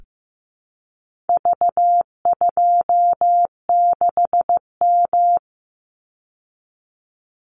Morse code, heard as a keyed tone, about 15 wpm, 700 Hz. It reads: V26M
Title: Call Signs Repeated 2x - 15wpm